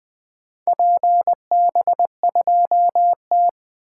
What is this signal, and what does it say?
Morse code, heard as a keyed tone, about 20 wpm, 700 Hz. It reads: PB2T
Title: 2023 SOTA Activator Call Signs - 20wpm